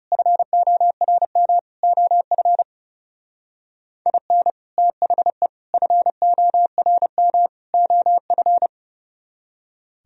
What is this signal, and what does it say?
Morse code, heard as a keyed tone, about 36 wpm, 700 Hz. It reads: FORM OF IN THE FORM OF
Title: Speed-Racing - Sets of 4 Words 30wpm